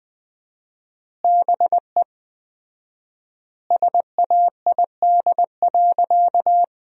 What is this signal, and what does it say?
Morse code, heard as a keyed tone, about 20 wpm, 700 Hz. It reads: BE SAID.
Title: 4x Word-Spacing - Sentences from Top 100 Words - 20wpm